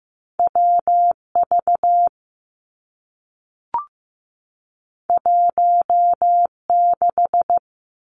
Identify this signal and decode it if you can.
Morse code, heard as a keyed tone, about 15 wpm, 700 Hz. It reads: W V 1 6
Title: Two Challenging Letter-Number Combinations 15wpm